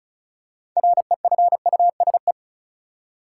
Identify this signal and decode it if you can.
Morse code, heard as a keyed tone, about 34 wpm, 700 Hz. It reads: REFUSE
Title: Six Letter Words - 35wpm